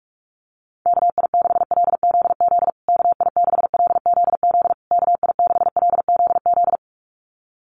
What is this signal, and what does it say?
Morse code, heard as a keyed tone, about 44 wpm, 700 Hz. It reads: KI6LZZ KI6LZZ KI6LZZ
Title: USA Call Signs Repeated 3x 45wpm